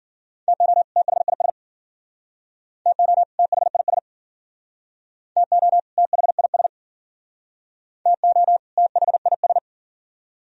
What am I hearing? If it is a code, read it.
Morse code, heard as a keyed tone, about 53 wpm, 700 Hz. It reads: TO THIS TO THIS TO THIS TO THIS